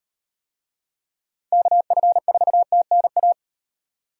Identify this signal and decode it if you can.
Morse code, heard as a keyed tone, about 38 wpm, 700 Hz. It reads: KF4TNA